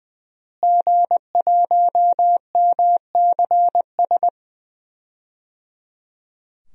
Morse code, heard as a keyed tone, about 20 wpm, 700 Hz. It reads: G1MCS
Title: DXCC-100 Call Signs - 5 Characters - Repeated 3x - 20wpm